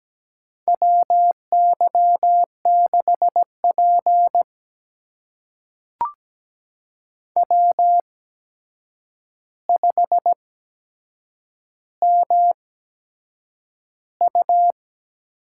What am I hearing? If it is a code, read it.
Morse code, heard as a keyed tone, about 17 wpm, spaced as for 7 wpm, 700 Hz. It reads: WY6P W5MU
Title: USA Call Signs - 8x Character Spacing - 17wpm